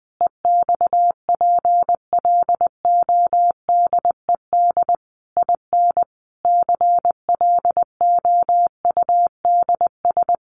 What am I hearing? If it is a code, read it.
Morse code, heard as a keyed tone, about 20 wpm, 700 Hz. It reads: EXPLODED IN CLOUDS
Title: A Sound Of Thunder 20wpm